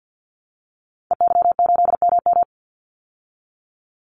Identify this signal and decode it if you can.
Morse code, heard as a keyed tone, about 49 wpm, 700 Hz. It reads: EY8MM
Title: Call Signs Repeated 3x - 50wpm